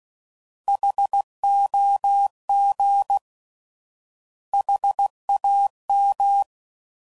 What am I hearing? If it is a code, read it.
Morse code, heard as a keyed tone, about 16 wpm, 800 Hz. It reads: HOG HAM
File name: Prefixes / By First Letter / Hs